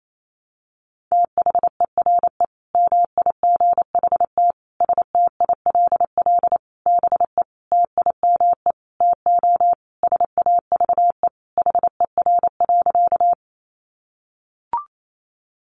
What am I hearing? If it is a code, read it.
Morse code, heard as a keyed tone, about 28 wpm, 700 Hz. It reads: THERE MIGHT STILL BE TIME TO SAVE HER.
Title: Sentences from Top 1100 Words 28wpm